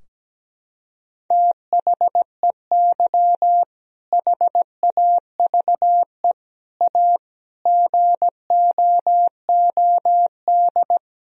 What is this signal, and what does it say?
Morse code, heard as a keyed tone, about 17 wpm, 700 Hz. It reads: THEY HAVE A GOOD
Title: Sentences from Top 1000 Words - Review Part 4 - 17wpm